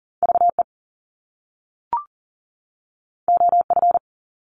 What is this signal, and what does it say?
Morse code, heard as a keyed tone, about 41 wpm, 700 Hz. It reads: VE OF
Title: Top Two Letter Combinations in Words 40wpm